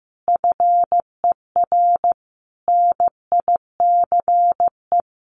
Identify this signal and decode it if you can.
Morse code, heard as a keyed tone, about 14 wpm, 700 Hz. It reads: FER NICE
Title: Common QSO Phrases 15wpm